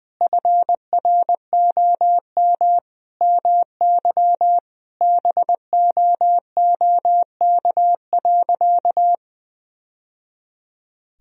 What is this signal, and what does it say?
Morse code, heard as a keyed tone, about 20 wpm, 700 Hz. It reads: FROM MY BOOK.
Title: Sentences from Top 1000 Words - Review Part 1 - 20wpm